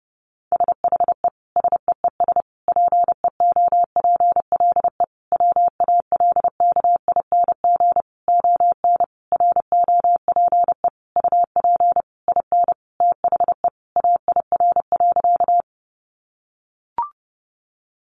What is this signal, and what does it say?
Morse code, heard as a keyed tone, about 30 wpm, 700 Hz. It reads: SHE SEES PEOPLE WALKING ON ROPE UP IN THE AIR.
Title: Easy Phrases 30wpm